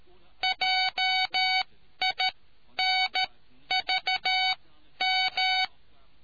Morse code, heard as a keyed tone, about 13 wpm, spaced as for 10 wpm, 750 Hz. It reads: JINVM